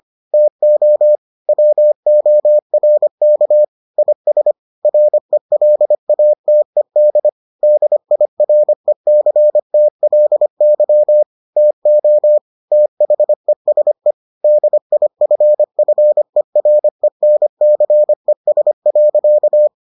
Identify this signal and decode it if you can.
Morse code, heard as a keyed tone, about 25 wpm, 600 Hz. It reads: TO WORK IS RELATED DIRECTLY TO THESE DIFFERENCES.